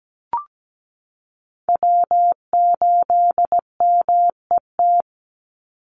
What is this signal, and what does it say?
Morse code, heard as a keyed tone, about 17 wpm, 700 Hz. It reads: W8MET
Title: Call Signs Repeated 2x - 17wpm